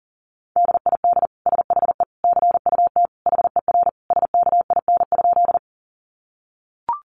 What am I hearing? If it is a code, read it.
Morse code, heard as a keyed tone, about 40 wpm, 700 Hz. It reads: DID SHE CUT HER SKIN?
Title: Sentences from Top 1300 Words 40wpm